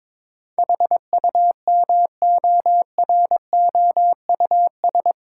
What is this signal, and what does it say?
Morse code, heard as a keyed tone, about 22 wpm, 700 Hz. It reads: HUMOROUS
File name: practice / 22wpm / Eight Letter Words - 22wpm